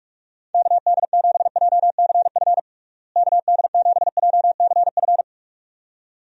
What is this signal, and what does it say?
Morse code, heard as a keyed tone, about 46 wpm, 700 Hz. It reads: KD7JCF KD7JCF
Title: USA Call Signs - 6 Characters - Repeated 2x - 45wpm